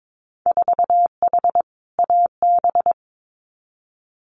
Morse code, heard as a keyed tone, about 22 wpm, 700 Hz. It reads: 4H AB
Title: Winter Field Day Exchanges Encore 22wpm